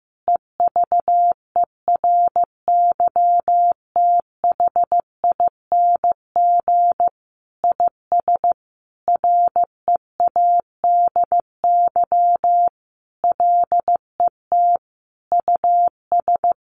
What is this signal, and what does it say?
Morse code, heard as a keyed tone, about 14 wpm, 700 Hz. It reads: EVERYTHING IS READY LET US
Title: Sentences from Top 700 Words 15wpm